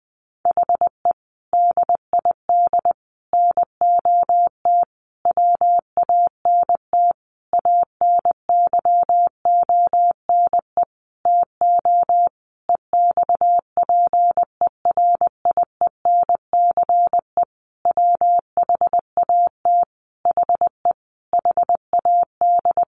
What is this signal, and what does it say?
Morse code, heard as a keyed tone, about 20 wpm, 700 Hz. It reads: HE DID NOT WANT ANYONE TO EXPERIENCE WHAT HE HAD
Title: Sentences from Top 1100 Words 20wpm